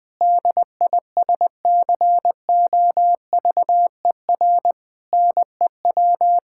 Morse code, heard as a keyed tone, about 20 wpm, 700 Hz. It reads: DISCOVER NEW
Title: Sentences with 5-Word Phrases - Spoken Beginning - 20wpm